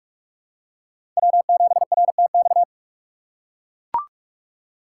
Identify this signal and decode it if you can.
Morse code, heard as a keyed tone, about 42 wpm, 700 Hz. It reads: W7RTX
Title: USA Call Signs - 5 Characters - 6x Character Spacing - 45wpm